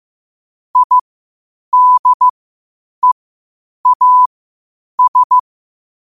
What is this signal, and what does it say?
Morse code, heard as a keyed tone, about 15 wpm, spaced as for 8 wpm, 1000 Hz. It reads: IDEAS
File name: qotdfiles / 8wpm